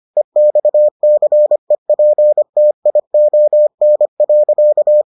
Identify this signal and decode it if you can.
Morse code, heard as a keyed tone, about 25 wpm, 600 Hz. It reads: EXCEPTION.